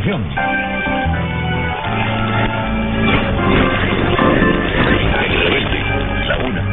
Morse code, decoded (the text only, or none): none